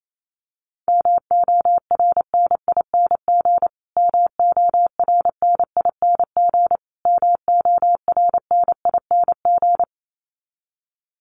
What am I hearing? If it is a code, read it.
Morse code, heard as a keyed tone, about 28 wpm, 700 Hz. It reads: MORNING MORNING MORNING
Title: Top 500 Words Repeated 28wpm